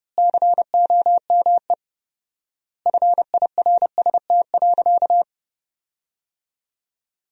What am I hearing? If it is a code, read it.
Morse code, heard as a keyed tone, about 32 wpm, 700 Hz. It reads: COME FIRST.
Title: 4x Word-Spacing - Sentences from Top 300 Words - 30wpm